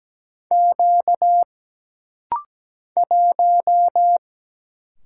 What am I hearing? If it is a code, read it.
Morse code, heard as a keyed tone, about 17 wpm, 700 Hz. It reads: Q 1
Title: Single Letter-Number Alt - Rapid-Fire 17wpm